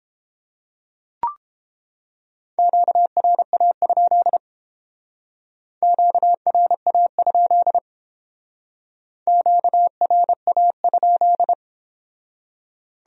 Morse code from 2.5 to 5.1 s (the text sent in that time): QRA?